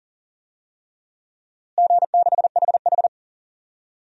Morse code, heard as a keyed tone, about 37 wpm, 700 Hz. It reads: G6HH
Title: DXCC-100 Call Signs - 4 Characters - Repeated 3x - 40wpm